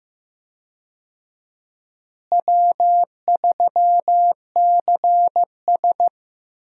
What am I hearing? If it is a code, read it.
Morse code, heard as a keyed tone, about 16 wpm, 700 Hz. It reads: W3CS
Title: USA Call Signs - 4 Characters - Repeated 2x - 15wpm